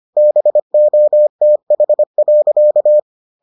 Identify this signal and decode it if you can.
Morse code, heard as a keyed tone, about 25 wpm, 600 Hz. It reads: BOTH.